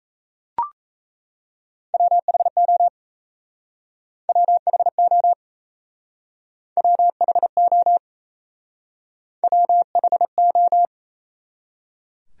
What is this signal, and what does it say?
Morse code, heard as a keyed tone, about 37 wpm, 700 Hz. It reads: WHO WHO WHO WHO